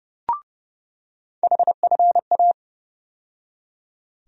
Morse code, heard as a keyed tone, about 32 wpm, 700 Hz. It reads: HFA
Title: USA Call Sign Postfixes - 3 Letters 30wpm